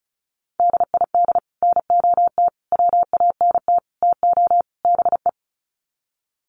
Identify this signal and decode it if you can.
Morse code, heard as a keyed tone, about 34 wpm, 700 Hz. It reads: DID NOT WANT TO BE
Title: Sets of 5 Words Encore 35wpm